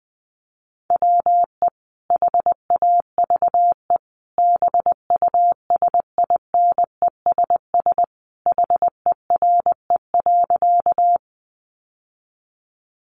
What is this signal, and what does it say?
Morse code, heard as a keyed tone, about 20 wpm, 700 Hz. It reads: WE HAVE BUSINESS HERE.
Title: Sentences from Top 600 Words 20wpm